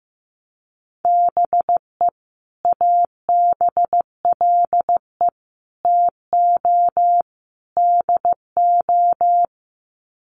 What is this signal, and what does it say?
Morse code, heard as a keyed tone, about 15 wpm, 700 Hz. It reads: BE ABLE TO DO
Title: Sets of 4 Words 15wpm